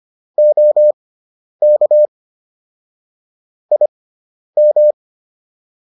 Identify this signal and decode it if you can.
Morse code, heard as a keyed tone, about 25 wpm, spaced as for 10 wpm, 600 Hz. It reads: OK IM